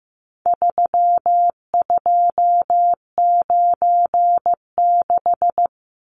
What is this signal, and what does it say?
Morse code, heard as a keyed tone, about 15 wpm, 700 Hz. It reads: 3296
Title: Numbers - 4-Digit 15wpm